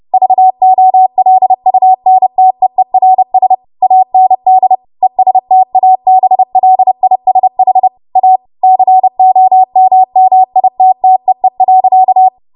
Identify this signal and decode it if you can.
Morse code, heard as a keyed tone, about 30 wpm, 750 Hz. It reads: VOLUNTEERS AND ESTABLISH A COMMITTEE.
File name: Morse / 30WPM